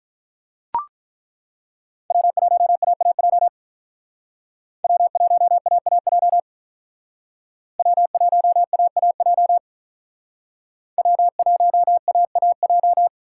Morse code, heard as a keyed tone, about 42 wpm, 700 Hz. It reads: W1AAJ W1AAJ W1AAJ W1AAJ
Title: Speed Racing - USA 1x3 Call Signs 35wpm